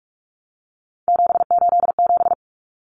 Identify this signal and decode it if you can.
Morse code, heard as a keyed tone, about 44 wpm, 700 Hz. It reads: 787
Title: Common Cut Numbers - 3-Digit 45wpm